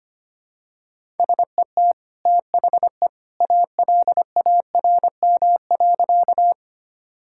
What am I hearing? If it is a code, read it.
Morse code, heard as a keyed tone, about 25 wpm, 700 Hz. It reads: SET THE ALARM.
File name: Easy Sentences with Hints - 25wpm